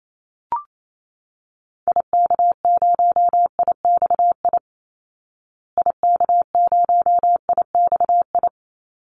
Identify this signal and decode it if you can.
Morse code, heard as a keyed tone, about 28 wpm, 700 Hz. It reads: IK0IXI IK0IXI
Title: Call Signs Repeated 3x - 28wpm